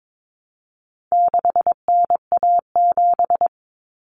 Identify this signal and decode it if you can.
Morse code, heard as a keyed tone, about 22 wpm, 700 Hz. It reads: 6NA7
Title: Common Cut Numbers - 4-Digit 22wpm